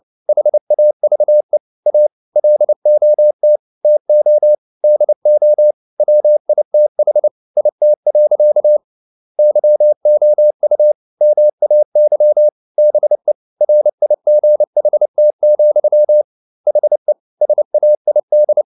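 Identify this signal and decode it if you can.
Morse code, heard as a keyed tone, about 29 wpm, 600 Hz. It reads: HAVE A LOT TO DO WITH IT. YOU MAY BE RIGHT, HE SAID
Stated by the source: Morse code at 29 wpm, sending Ken Follett's novel Never in English, from CW med Gnister.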